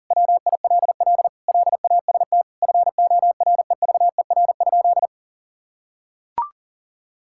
Morse code, heard as a keyed tone, about 37 wpm, 700 Hz. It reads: WILL LAST FOREVER?
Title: Sentences from Top 1600 Words 40wpm